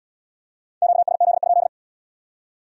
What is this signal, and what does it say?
Morse code, heard as a keyed tone, about 72 wpm, 700 Hz. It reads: QSZ?